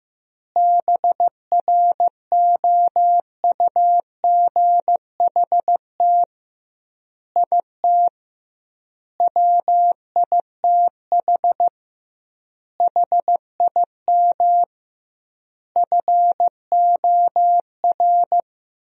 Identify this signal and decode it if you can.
Morse code, heard as a keyed tone, about 16 wpm, 700 Hz. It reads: BROUGHT IT WITH HIM FOR
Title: 2x Word-Spacing - Sentences from Top 500 Words - 15wpm